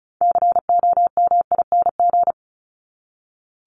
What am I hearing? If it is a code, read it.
Morse code, heard as a keyed tone, about 34 wpm, 700 Hz. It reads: COMING
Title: ING Suffix - Top 100 Words with Base Words 35wpm